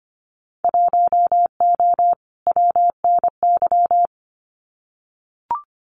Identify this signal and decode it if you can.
Morse code, heard as a keyed tone, about 25 wpm, 700 Hz. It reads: 1O WNY
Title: Winter Field Day Exchanges Encore 25wpm